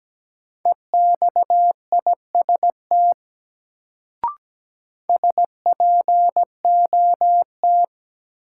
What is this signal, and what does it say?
Morse code, heard as a keyed tone, about 17 wpm, 700 Hz. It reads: EXIST SPOT
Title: Sentences from Top 1100 Words - New Word Drill Alt - 17wpm